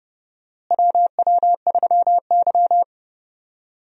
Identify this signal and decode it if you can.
Morse code, heard as a keyed tone, about 32 wpm, 700 Hz. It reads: WW3Y
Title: USA Call Signs - 4 Characters - Repeated 3x - 30wpm